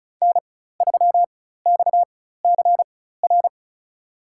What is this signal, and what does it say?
Morse code, heard as a keyed tone, about 36 wpm, spaced as for 16 wpm, 700 Hz. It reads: N3XCR
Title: USA Call Signs - 5 Characters - 4x Character Spacing - 35wpm